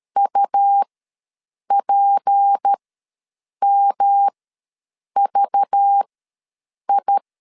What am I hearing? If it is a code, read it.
Morse code, heard as a keyed tone, about 13 wpm, spaced as for 7 wpm, 800 Hz. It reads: UPMVI
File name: LEZIONI / 6 PKX0 / 6 P ws 12wpm 3117